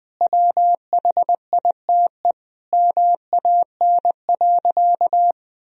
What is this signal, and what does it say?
Morse code, heard as a keyed tone, about 20 wpm, 700 Hz. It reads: WHITE MAN.
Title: Sentences from Top 1000 Words - Review Part 1 - 20wpm